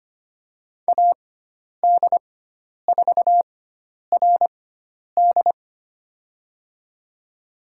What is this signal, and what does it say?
Morse code, heard as a keyed tone, about 25 wpm, spaced as for 10 wpm, 700 Hz. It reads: AD4RD
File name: Lesson 16g - Letter c - Call Signs 25wpm